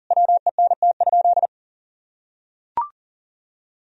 Morse code, heard as a keyed tone, about 40 wpm, 700 Hz. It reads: WENT?
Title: Sentences from Top 1000 Words - Review Part 3 - 40wpm